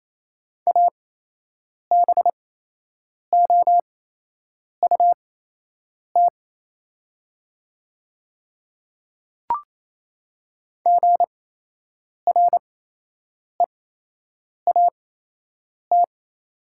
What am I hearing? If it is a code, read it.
Morse code, heard as a keyed tone, about 27 wpm, spaced as for 8 wpm, 700 Hz. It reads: ABOUT GREAT
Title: 8x Character Spacing - Top 200 Words - 28wpm